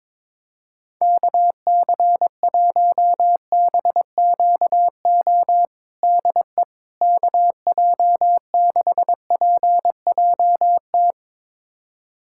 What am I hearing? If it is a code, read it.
Morse code, heard as a keyed tone, about 21 wpm, 700 Hz. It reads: KC1BQO DE KJ6PJT
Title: USA Callsign DE Callsign - 6 Characters - 22wpm